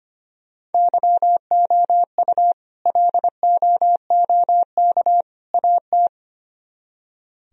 Morse code, heard as a keyed tone, about 33 wpm, 700 Hz. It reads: YOU LOOK AT